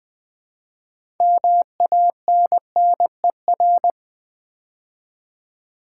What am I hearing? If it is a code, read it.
Morse code, heard as a keyed tone, about 20 wpm, 700 Hz. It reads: MANNER
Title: 6 Letter Words - First 3 Letters Given - 20wpm